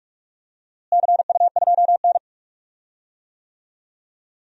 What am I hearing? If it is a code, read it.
Morse code, heard as a keyed tone, about 46 wpm, 700 Hz. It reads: CU2N
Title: DXCC-100 Call Signs - 4 Characters - Repeated 2x - 45wpm